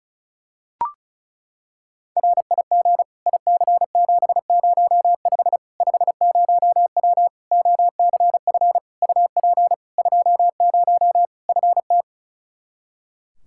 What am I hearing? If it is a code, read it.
Morse code, heard as a keyed tone, about 36 wpm, 700 Hz. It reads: RIG IC705 50W OCF UP 20 FT
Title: Practice QSO - Exchange 2 - Info - 35wpm